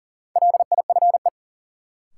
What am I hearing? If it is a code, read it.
Morse code, heard as a keyed tone, about 40 wpm, 700 Hz. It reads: LIFE
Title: Sets of 5 Words - New Word Drill - 40wpm